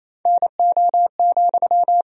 Morse code, heard as a keyed tone, about 28 wpm, 700 Hz. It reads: NO,
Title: Sentences from Top 1000 Words - Review Part 2 - 28wpm